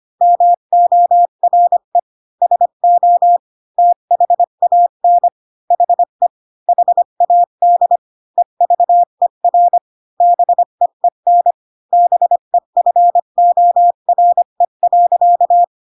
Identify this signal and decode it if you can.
Morse code, heard as a keyed tone, about 25 wpm, 700 Hz. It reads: MORE SO THAN HE HAD EVER BEEN BEFORE.
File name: Pinocchio / Pinocchio ch3 20 25 ik1hge